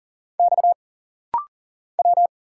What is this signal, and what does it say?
Morse code, heard as a keyed tone, about 40 wpm, 700 Hz. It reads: X W
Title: Single Letter-Number - Rapid-Fire 40wpm